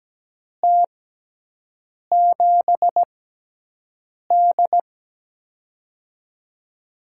Morse code, heard as a keyed tone, about 17 wpm, spaced as for 6 wpm, 700 Hz. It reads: T7D
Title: USA Call Signs - 4 Characters - 6x Character Spacing - 17wpm